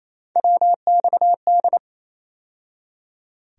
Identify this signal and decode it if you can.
Morse code, heard as a keyed tone, about 28 wpm, 700 Hz. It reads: WXD
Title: USA Call Sign Postfixes - 3 Letters 28wpm